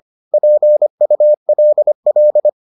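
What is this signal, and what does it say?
Morse code, heard as a keyed tone, about 25 wpm, 600 Hz. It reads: PULL
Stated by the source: Morse edition of the Foundations of Amateur Radio episodes.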